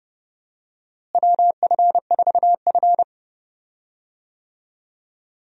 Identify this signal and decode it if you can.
Morse code, heard as a keyed tone, about 31 wpm, 700 Hz. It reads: WF4F
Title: USA Call Signs - 4 Characters - Repeated 3x - 30wpm